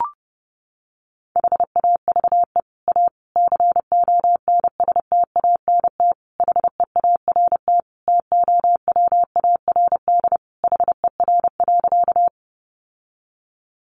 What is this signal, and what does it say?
Morse code, heard as a keyed tone, about 30 wpm, 700 Hz. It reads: HAVE A CONSTANT HEART TOWARD HER.
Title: Sentences from Top 1500 Words 30wpm